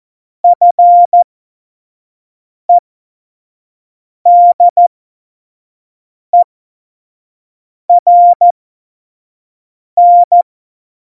Morse code, heard as a keyed tone, about 14 wpm, spaced as for 5 wpm, 700 Hz. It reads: FEDERN
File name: Morsekurs / Witze / Witz 23. Oktober 2018, 25 F70